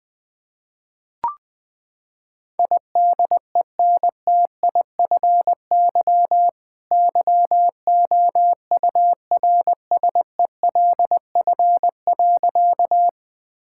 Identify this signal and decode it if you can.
Morse code, heard as a keyed tone, about 20 wpm, 700 Hz. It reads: IDENTIFY YOURSELF.
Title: Sentences from Top 1000 Words - Review Part 6 - 20wpm